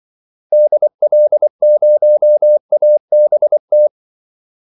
Morse code, heard as a keyed tone, about 24 wpm, 600 Hz. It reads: DL0ABT
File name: Callsigns-24wpm